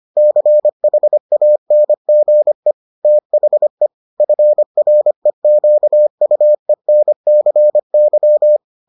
Morse code, heard as a keyed tone, about 25 wpm, 600 Hz. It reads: CHANGE THE FREQUENCY